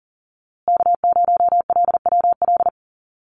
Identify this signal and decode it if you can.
Morse code, heard as a keyed tone, about 40 wpm, 700 Hz. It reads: K0LWL
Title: USA Call Signs Repeated 3x 40wpm